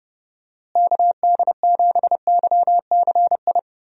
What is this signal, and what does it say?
Morse code, heard as a keyed tone, about 30 wpm, 700 Hz. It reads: KD7YCI
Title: USA Call Signs - 6 Characters - Repeated 2x - 30wpm